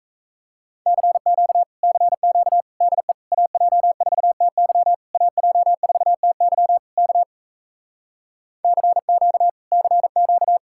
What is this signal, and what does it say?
Morse code, heard as a keyed tone, about 38 wpm, 700 Hz. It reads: CQ CQ DE AJ4TY AJ4TY K CQ CQ